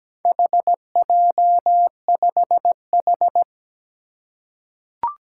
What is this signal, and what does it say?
Morse code, heard as a keyed tone, about 17 wpm, 700 Hz. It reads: HJ5H
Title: Four Challenging Letter-Number Combinations 17wpm